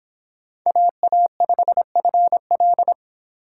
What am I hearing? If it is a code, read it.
Morse code, heard as a keyed tone, about 25 wpm, 700 Hz. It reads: AA5FL